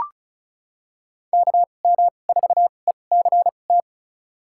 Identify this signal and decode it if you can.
Morse code, heard as a keyed tone, about 34 wpm, spaced as for 25 wpm, 700 Hz. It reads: KM4ECT
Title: USA Call Signs - 6 Characters - 2x Character Spacing - 35wpm